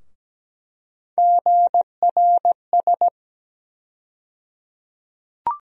Morse code, heard as a keyed tone, about 17 wpm, 700 Hz. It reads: GRS
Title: USA Call Sign Postfixes - 3 Letters 17wpm